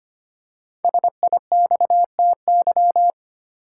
Morse code, heard as a keyed tone, about 25 wpm, 700 Hz. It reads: SIXTY
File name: 5 Letter Words - First 3 Letters Given - 25wpm